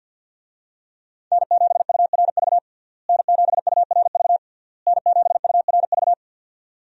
Morse code, heard as a keyed tone, about 52 wpm, 700 Hz. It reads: N7URV N7URV N7URV
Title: USA Call Signs - 5 Characters - Repeated 3x - 50wpm